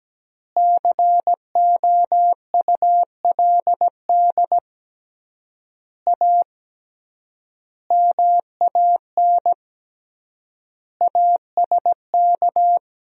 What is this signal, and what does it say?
Morse code, heard as a keyed tone, about 17 wpm, 700 Hz. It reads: COULD A MAN ASK
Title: 3x Word-Spacing - Sentences from Top 200 Words - 17wpm